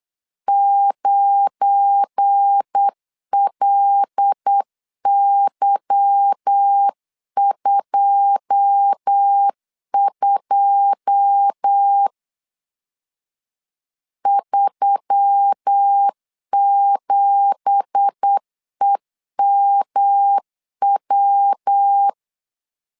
Morse code, heard as a keyed tone, about 8 wpm, 800 Hz. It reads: 9LY22 37EMW
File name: CW / DET / 1 9 1500C S C 8wpm 5653